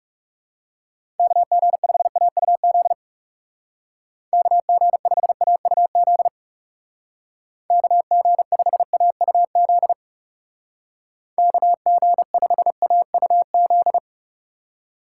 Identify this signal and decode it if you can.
Morse code, heard as a keyed tone, about 38 wpm, 700 Hz. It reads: KG5AUZ KG5AUZ KG5AUZ KG5AUZ